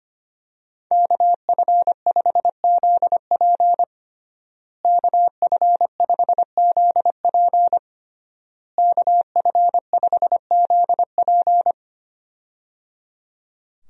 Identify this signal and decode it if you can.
Morse code, heard as a keyed tone, about 25 wpm, 700 Hz. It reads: KF5ZP KF5ZP KF5ZP
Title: USA Call Signs - 5 Characters - Repeated 3x - 25wpm